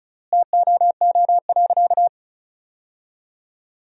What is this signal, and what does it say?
Morse code, heard as a keyed tone, about 36 wpm, 700 Hz. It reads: TOO.
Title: 3x Word-Spacing - Sentences from Top 400 Words - 35wpm